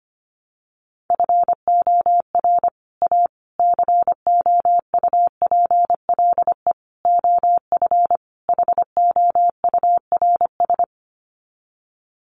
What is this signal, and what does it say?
Morse code, heard as a keyed tone, about 25 wpm, 700 Hz. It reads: FOR A COUPLE OF HOURS
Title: Sets of 5 Words Encore2 25wpm